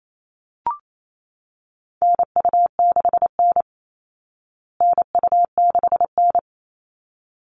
Morse code, heard as a keyed tone, about 28 wpm, 700 Hz. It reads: NU6N NU6N
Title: Call Signs Repeated 3x - 28wpm